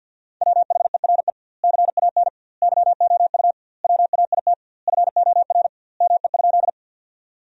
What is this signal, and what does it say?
Morse code, heard as a keyed tone, about 50 wpm, 700 Hz. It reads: WHERE CAN YOU WAIT FOR ME?
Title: 2x Word-Spacing - Sentences from Top 500 Words - 50wpm